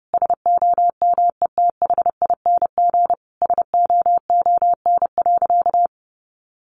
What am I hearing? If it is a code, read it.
Morse code, heard as a keyed tone, about 30 wpm, 700 Hz. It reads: SOMETHING SOON.
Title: Sentences from Top 1500 Words 30wpm